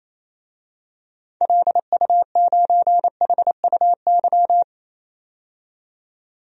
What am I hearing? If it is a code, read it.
Morse code, heard as a keyed tone, about 28 wpm, 700 Hz. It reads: LU9HUY
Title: DXCC-100 Call Signs - 6 Characters - 28wpm